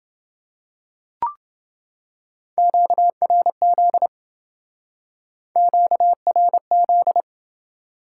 Morse code, heard as a keyed tone, about 28 wpm, 700 Hz. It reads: QRZ QRZ